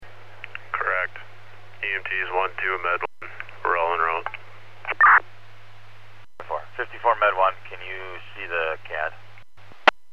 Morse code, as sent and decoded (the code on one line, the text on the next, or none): none